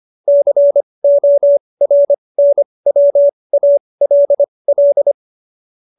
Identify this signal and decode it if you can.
Morse code, heard as a keyed tone, about 25 wpm, spaced as for 20 wpm, 570 Hz. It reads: CORNWALL
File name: QSO310 20